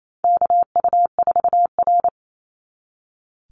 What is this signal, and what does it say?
Morse code, heard as a keyed tone, about 28 wpm, 700 Hz. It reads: KU4R
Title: NA SOTA Chaser Callsigns 28wpm